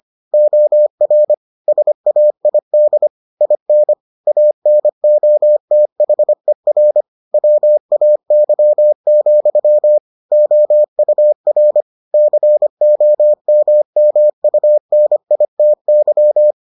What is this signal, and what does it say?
Morse code, heard as a keyed tone, about 25 wpm, 600 Hz. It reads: OR SAID IN ANOTHER WAY, OUR COMMUNITY